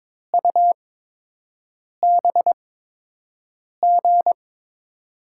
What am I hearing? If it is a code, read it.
Morse code, heard as a keyed tone, about 22 wpm, spaced as for 6 wpm, 700 Hz. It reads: UBG
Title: USA Call Signs - 6 Characters - 8x Character Spacing - 22wpm